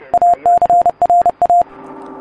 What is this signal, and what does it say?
Morse code, heard as a keyed tone, about 30 wpm, 700 Hz. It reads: ACRA